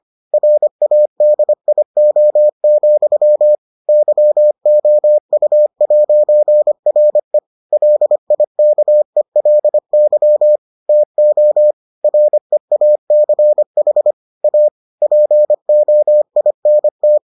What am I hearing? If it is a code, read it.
Morse code, heard as a keyed tone, about 25 wpm, 600 Hz. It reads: RADIO, YOU'RE LIKELY TO REACH A POINT